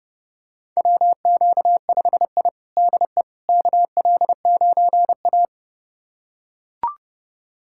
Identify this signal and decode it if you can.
Morse code, heard as a keyed tone, about 30 wpm, 700 Hz. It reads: WQ5I DE KL9A
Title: USA Callsign DE Callsign - 4 Characters - 30wpm